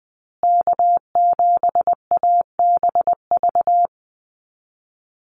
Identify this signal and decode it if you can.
Morse code, heard as a keyed tone, about 20 wpm, 700 Hz. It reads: K7ABV
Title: Call Signs Repeated 3x - 20wpm